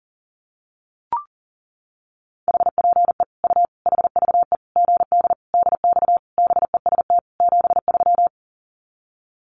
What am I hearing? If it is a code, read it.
Morse code, heard as a keyed tone, about 37 wpm, 700 Hz. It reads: HPE U HVE GD DX BEST 73
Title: Common QSO Phrases 40wpm